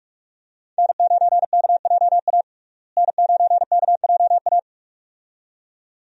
Morse code, heard as a keyed tone, about 42 wpm, 700 Hz. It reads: N9KJA N9KJA
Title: USA Call Signs - 5 Characters - Repeated 2x - 45wpm